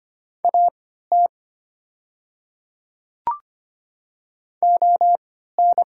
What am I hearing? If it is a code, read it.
Morse code, heard as a keyed tone, about 25 wpm, spaced as for 14 wpm, 690 Hz. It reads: AT ON
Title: 3x Character Spacing - Top 100 Words - 25wpm